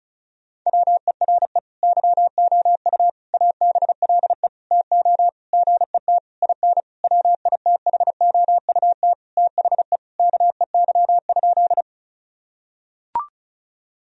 Morse code, heard as a keyed tone, about 34 wpm, 700 Hz. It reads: WERE YOU ABLE TO GET IN WITHOUT THE KEY?
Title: Sentences from Top 1000 Words - Review Part 4 - 35wpm